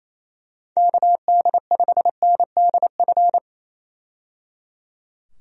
Morse code, heard as a keyed tone, about 28 wpm, 700 Hz. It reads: KD5NDF
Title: USA Call Signs - 6 Characters - 28wpm